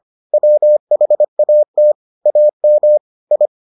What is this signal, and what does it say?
Morse code, heard as a keyed tone, about 25 wpm, 600 Hz. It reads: WHAT AM I